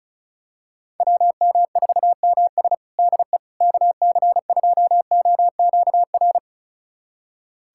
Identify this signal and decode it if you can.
Morse code, heard as a keyed tone, about 36 wpm, 700 Hz. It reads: WM4MS DE KC2OQR
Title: USA Callsign DE Callsign 35wpm